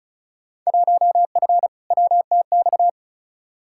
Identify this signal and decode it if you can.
Morse code, heard as a keyed tone, about 35 wpm, 700 Hz. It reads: 1F WTX
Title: Field Day Exchanges Encore - 35wpm